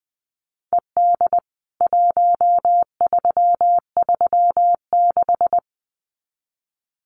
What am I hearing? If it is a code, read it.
Morse code, heard as a keyed tone, about 20 wpm, 700 Hz. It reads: ED 1336
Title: CWT Exchange 20wpm